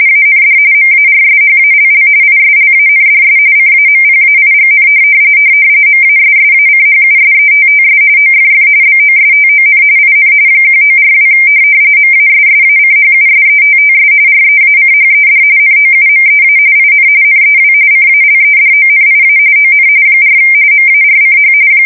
RTTY, about 45.45 baud. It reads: AND MARSHALL ISLANDS CAMPAIGN MASSACHUSETTS OFF BOSTON IN 1942 MASSACHUSETTS REACHED NOUMEA IN NEW CALEDONIA ON 4 MARCH
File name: fsk170 / USS-Massachusetts